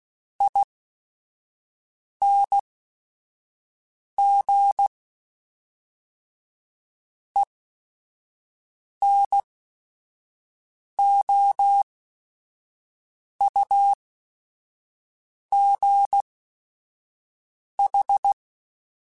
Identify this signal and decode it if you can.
Morse code, heard as a keyed tone, about 16 wpm, spaced as for 5 wpm, 800 Hz. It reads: ING ENOUGH
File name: Technical Subjects / Technical Subjects-mp3s / XC @5